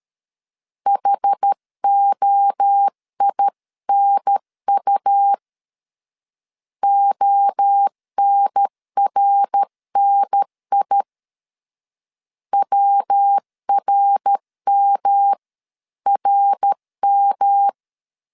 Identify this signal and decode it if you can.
Morse code, heard as a keyed tone, about 13 wpm, 800 Hz. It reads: HOINU ONRNI WRMRM
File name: CW / DET / 1 3 1500C S A 12wpm 2905